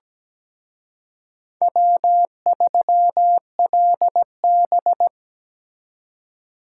Morse code, heard as a keyed tone, about 20 wpm, 700 Hz. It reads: W3LB
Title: Speed Racing - USA 1x2 Call Signs 17wpm